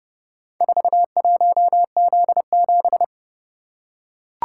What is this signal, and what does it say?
Morse code, heard as a keyed tone, about 29 wpm, 700 Hz. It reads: 41Z7